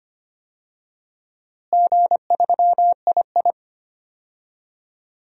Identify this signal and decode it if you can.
Morse code, heard as a keyed tone, about 25 wpm, 700 Hz. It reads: G3II
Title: DXCC-100 Call Signs - 4 Characters - Repeated 3x - 25wpm